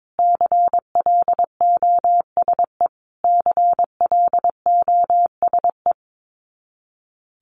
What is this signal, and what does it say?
Morse code, heard as a keyed tone, about 22 wpm, 700 Hz. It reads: CLOSE CLOSE
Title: Top 500 Words Repeated 22wpm